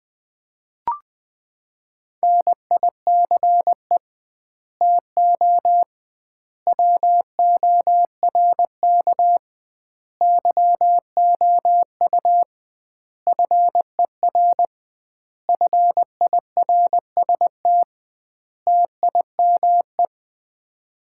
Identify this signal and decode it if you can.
Morse code, heard as a keyed tone, about 20 wpm, 700 Hz. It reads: NICE TO WORK YOU FER FIRST TIME
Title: Common QSO Phrases - 2x Word-Spacing - 20wpm